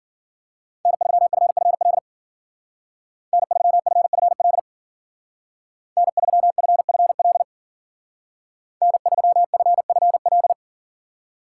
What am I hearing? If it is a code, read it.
Morse code, heard as a keyed tone, about 50 wpm, 700 Hz. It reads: N3FFL N3FFL N3FFL N3FFL
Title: Speed Racing - USA 1x3 Call Signs 40wpm